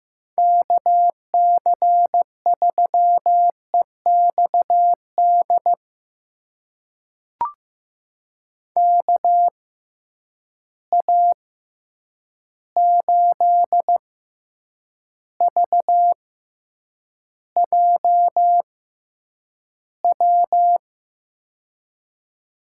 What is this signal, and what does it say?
Morse code, heard as a keyed tone, about 15 wpm, spaced as for 8 wpm, 700 Hz. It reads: KC3EXD KA8VJW
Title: USA Call Signs - 6 Characters - 6x Character Spacing - 15wpm